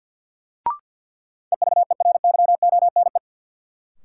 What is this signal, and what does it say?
Morse code, heard as a keyed tone, about 51 wpm, 700 Hz. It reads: EVERYONE
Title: Sentences from Top 800 Words - New Word Drill - 50wpm